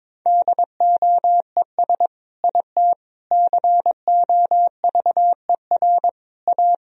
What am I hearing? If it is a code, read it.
Morse code, heard as a keyed tone, about 22 wpm, 700 Hz. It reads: DOES IT COVER A
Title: Sentences from Top 1000 Words - Review Part 2 - 22wpm